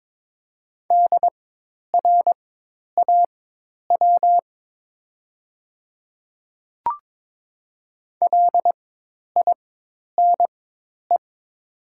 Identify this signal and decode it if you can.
Morse code, heard as a keyed tone, about 22 wpm, spaced as for 10 wpm, 700 Hz. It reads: DRAW LINE
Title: 4x Character Spacing - Top 300 Words - 22wpm